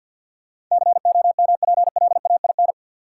Morse code, heard as a keyed tone, about 50 wpm, 700 Hz. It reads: COMPLAIN
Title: Sentences from Top 1300 Words - New Word Drill Alt - 50wpm